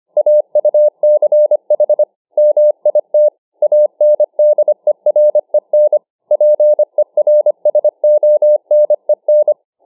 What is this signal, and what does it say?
Morse code, heard as a keyed tone, about 25 wpm, 600 Hz. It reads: AUCH MIT ANDEREN PERSONEN